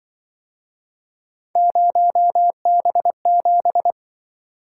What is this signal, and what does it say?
Morse code, heard as a keyed tone, about 25 wpm, 700 Hz. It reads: 0B7